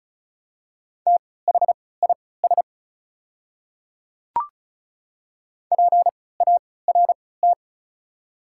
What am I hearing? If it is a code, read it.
Morse code, heard as a keyed tone, about 36 wpm, spaced as for 20 wpm, 700 Hz. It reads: THIS PART
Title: 3x Character Spacing - Top 200 Words - 35wpm